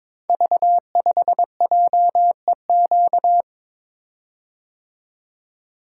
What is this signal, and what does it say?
Morse code, heard as a keyed tone, about 22 wpm, 690 Hz. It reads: V5JEQ
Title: Code Groups - 5 Letter:Numbers - 22wpm